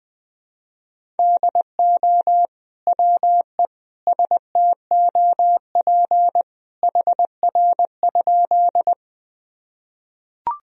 Morse code, heard as a keyed tone, about 20 wpm, 700 Hz. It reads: DO WE STOP HR?
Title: Sentences from Top 300 Words and Abbreviations 20wpm